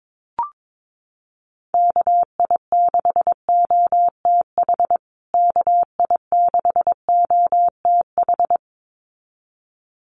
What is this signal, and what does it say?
Morse code, heard as a keyed tone, about 22 wpm, 700 Hz. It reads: KI6OTH KI6OTH
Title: USA Call Signs Repeated 2x 22wpm